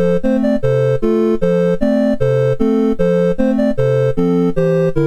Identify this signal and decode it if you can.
There is no code.